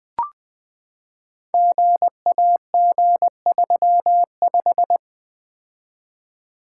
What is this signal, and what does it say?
Morse code, heard as a keyed tone, about 21 wpm, 700 Hz. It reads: GAG35
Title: Code Groups - 5 Letter:Numbers - 20wpm